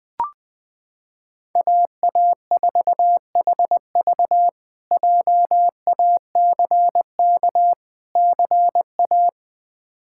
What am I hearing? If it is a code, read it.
Morse code, heard as a keyed tone, about 20 wpm, 700 Hz. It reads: AA4HV JACK CA
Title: K1USN SST QSO Practice - 20wpm